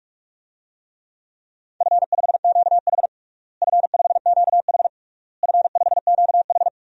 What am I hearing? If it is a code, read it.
Morse code, heard as a keyed tone, about 46 wpm, 700 Hz. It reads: F5QH F5QH F5QH
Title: DXCC-100 Call Signs - 4 Characters - Repeated 3x - 45wpm